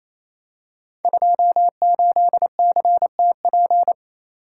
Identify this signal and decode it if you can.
Morse code, heard as a keyed tone, about 27 wpm, 690 Hz. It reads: 28CTP
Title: Code Groups - 5 Letter:Numbers - 28wpm